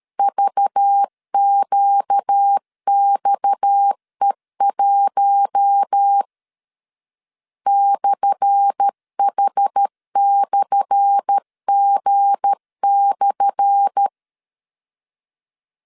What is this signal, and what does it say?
Morse code, heard as a keyed tone, about 13 wpm, 800 Hz. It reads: VQXE1 /H/G/
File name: LEZIONI / 10 SIGNS / 10A SLASH 12wpm 2314